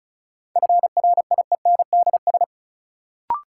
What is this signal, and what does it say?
Morse code, heard as a keyed tone, about 36 wpm, 700 Hz. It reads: FRIENDS
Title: Sentences from Top 1000 Words - Review Part 5 - New Word Drill - 35wpm